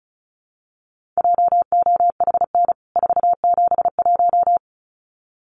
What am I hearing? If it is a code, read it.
Morse code, heard as a keyed tone, about 34 wpm, 700 Hz. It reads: JOHN 471
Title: CWT Exchange 35wpm